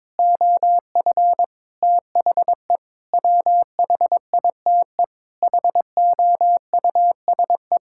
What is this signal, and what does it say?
Morse code, heard as a keyed tone, about 22 wpm, 700 Hz. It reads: OF THE WHITE HOUSE
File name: practice / 22wpm / Sets of 8 Words - 22wpm